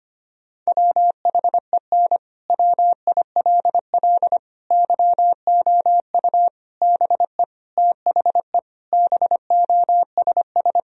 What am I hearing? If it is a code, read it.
Morse code, heard as a keyed tone, about 25 wpm, 700 Hz. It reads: WHEN WILL YOU BE THE BOSS
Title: Sentences from Top 1000 Words - Review Part 4 - 25wpm